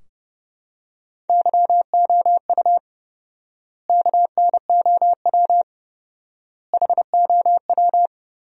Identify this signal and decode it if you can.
Morse code, heard as a keyed tone, about 30 wpm, 700 Hz. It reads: YOU KNOW HOW
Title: 4x Word-Spacing - Sentences from Top 100 Words - 30wpm